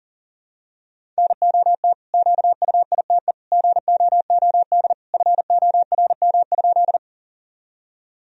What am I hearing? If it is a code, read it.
Morse code, heard as a keyed tone, about 37 wpm, 700 Hz. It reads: NOT QUITE GOOD FORM?
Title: Sentences from Top 1000 Words - Review Part 1 - 40wpm